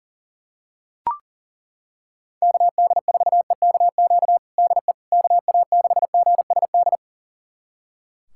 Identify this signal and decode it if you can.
Morse code, heard as a keyed tone, about 37 wpm, 700 Hz. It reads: KD4EKQ DE KA6GSD
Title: USA Callsign DE Callsign - 6 Characters - 40wpm